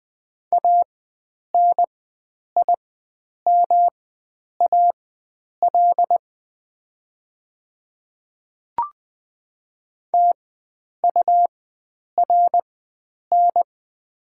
Morse code, heard as a keyed tone, about 20 wpm, spaced as for 9 wpm, 700 Hz. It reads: ANIMAL TURN
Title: 4x Character Spacing - Top 500 Words - 20wpm